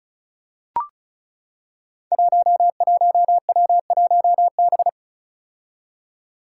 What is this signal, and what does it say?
Morse code, heard as a keyed tone, about 34 wpm, 700 Hz. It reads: 11W1B
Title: Five Challenging Letter-Number Combinations 35wpm